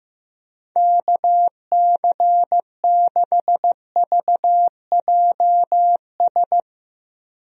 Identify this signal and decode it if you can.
Morse code, heard as a keyed tone, about 15 wpm, 700 Hz. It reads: KC6VJS
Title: USA Call Signs - 6 Characters - Repeated 2x - 15wpm